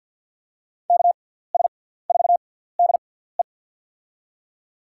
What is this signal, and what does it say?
Morse code, heard as a keyed tone, about 52 wpm, spaced as for 17 wpm, 700 Hz. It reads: KS4DE
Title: USA Call Signs - 5 Characters - 6x Character Spacing - 50wpm